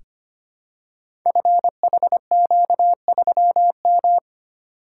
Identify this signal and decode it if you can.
Morse code, heard as a keyed tone, about 25 wpm, 700 Hz. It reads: FHQ3M
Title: Code Groups - 5 Letter:Numbers - 25wpm